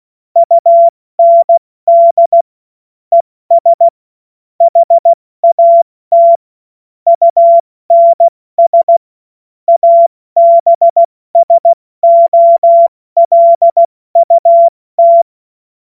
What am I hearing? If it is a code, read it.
Morse code, heard as a keyed tone, about 16 wpm, 680 Hz. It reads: UND ES HAT UNS ABSOLUT
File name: Morsekurs / Witze / Witz 03. Mai 2022 70 F80